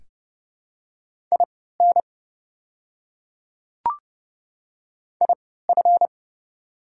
Morse code, heard as a keyed tone, about 30 wpm, spaced as for 17 wpm, 700 Hz. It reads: IN IF
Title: 3x Character Spacing - Top 200 Words - 30wpm